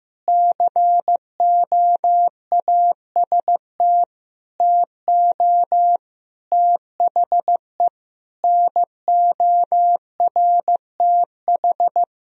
Morse code, heard as a keyed tone, about 16 wpm, 700 Hz. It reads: COAST TO THE NORTH
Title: Sentences from Top 1000 Words - Review Part 4 - 15wpm